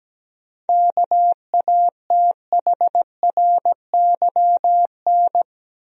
Morse code, heard as a keyed tone, about 17 wpm, 700 Hz. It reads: KATHRYN
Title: USA Common Names - 17wpm